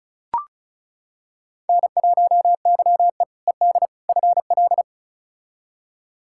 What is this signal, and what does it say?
Morse code, heard as a keyed tone, about 36 wpm, 700 Hz. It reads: N1YE ED FL
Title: K1USN SST QSO Practice - 35wpm